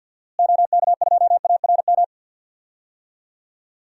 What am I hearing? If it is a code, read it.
Morse code, heard as a keyed tone, about 52 wpm, 700 Hz. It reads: OK2ARM
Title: DXCC-100 Call Signs - 6 Characters - Repeated 3x - 50wpm